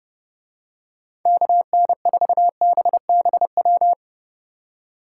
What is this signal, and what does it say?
Morse code, heard as a keyed tone, about 31 wpm, 700 Hz. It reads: KN4BBW